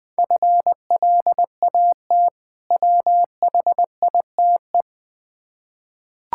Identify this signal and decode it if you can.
Morse code, heard as a keyed tone, about 20 wpm, 700 Hz. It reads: FLAT WHITE
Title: Difficult Words with Hints - 20wpm